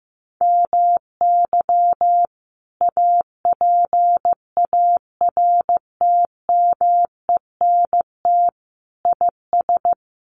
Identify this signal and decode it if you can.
Morse code, heard as a keyed tone, about 15 wpm, 700 Hz. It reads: MY APARTMENT IS
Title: Sentences from Top 900 Words 15wpm